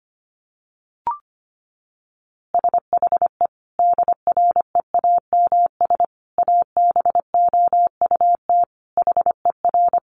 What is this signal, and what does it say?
Morse code, heard as a keyed tone, about 25 wpm, 700 Hz. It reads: SHE DREAMS ABOUT HER
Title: Easy Phrases 25wpm